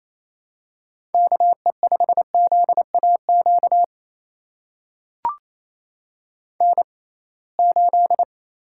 Morse code, heard as a keyed tone, about 28 wpm, spaced as for 21 wpm, 700 Hz. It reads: KE5ZAQ N8
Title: USA Call Signs - 6x Character Spacing - 28wpm